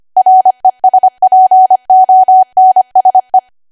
Morse code, heard as a keyed tone, about 25 wpm, 750 Hz. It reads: RESPONSE
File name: Archive / 25 WPM / 25WPM